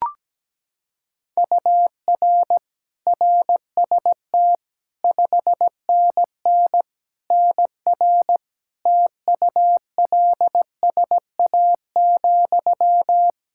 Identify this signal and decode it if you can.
Morse code, heard as a keyed tone, about 17 wpm, 700 Hz. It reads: UR RST 5NN NR TULSA,
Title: Practice QSO - Exchange 1 - Beginning - 17wpm